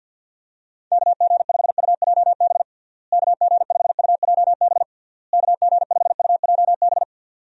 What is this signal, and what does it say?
Morse code, heard as a keyed tone, about 51 wpm, 700 Hz. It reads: KG5UJB KG5UJB KG5UJB
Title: USA Call Signs - 6 Characters - Repeated 3x - 50wpm